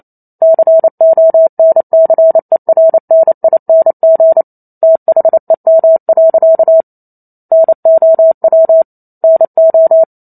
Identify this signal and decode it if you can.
Morse code, heard as a keyed tone, about 28 wpm, 650 Hz. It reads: CONCERNING THEM. NOW NO